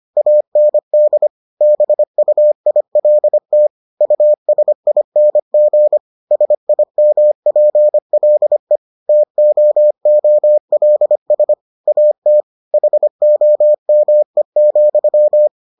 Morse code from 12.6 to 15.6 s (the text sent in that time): HOME,